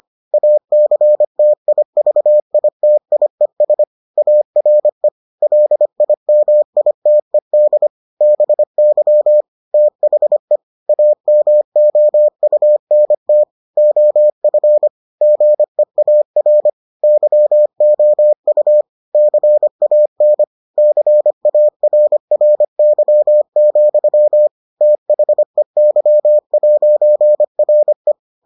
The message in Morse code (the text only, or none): ACTIVITIES ARE LIMITED BY THE AMOUNT OF GEAR YOU CAN CARRY, THEY'RE